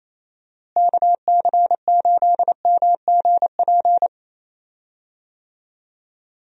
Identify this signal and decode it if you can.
Morse code, heard as a keyed tone, about 28 wpm, 700 Hz. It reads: KC8MGP
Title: USA Call Signs - 6 Characters - Repeated 2x - 28wpm